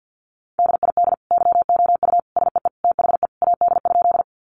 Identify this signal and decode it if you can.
Morse code, heard as a keyed tone, about 50 wpm, 700 Hz. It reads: DID YOU SEE THE AD?
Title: Sentences from Top 1300 Words 50wpm